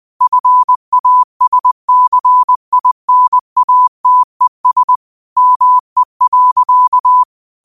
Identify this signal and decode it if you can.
Morse code, heard as a keyed tone, about 20 wpm, 1000 Hz. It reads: FASCINATES ME.
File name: qotdfiles / 20wpm